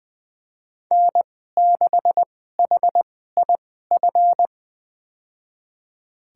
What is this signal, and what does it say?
Morse code, heard as a keyed tone, about 20 wpm, spaced as for 15 wpm, 700 Hz. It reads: N6HIF
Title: USA Call Signs - 5 Characters - 2x Character Spacing - 20wpm